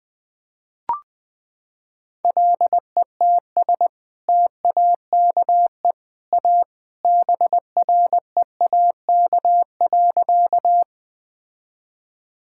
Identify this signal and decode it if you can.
Morse code, heard as a keyed tone, about 20 wpm, 700 Hz. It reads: LETS TAKE A BREAK.
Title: Sentences with 2-Word Phrases - 20wpm